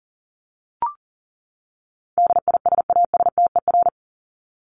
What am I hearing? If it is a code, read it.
Morse code, heard as a keyed tone, about 40 wpm, 700 Hz. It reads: DISASTER
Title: Top 1501-1750 Words 40wpm